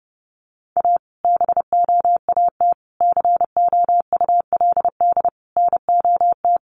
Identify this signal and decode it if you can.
Morse code, heard as a keyed tone, about 30 wpm, 700 Hz. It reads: A BOAT COULD NOT
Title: Sentences from Top 1400 Words 30wpm